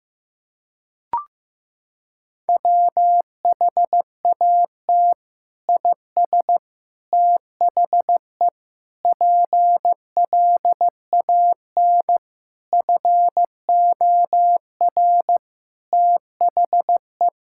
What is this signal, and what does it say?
Morse code, heard as a keyed tone, about 16 wpm, 700 Hz. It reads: WHAT IS THE PLAN FOR THE
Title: Sentences from Top 1000 Words - Review Part 5 - 15wpm